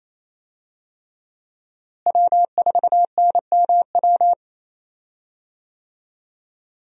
Morse code, heard as a keyed tone, about 28 wpm, 700 Hz. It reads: W4NMW
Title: USA Call Signs - 5 Characters - Repeated 3x - 28wpm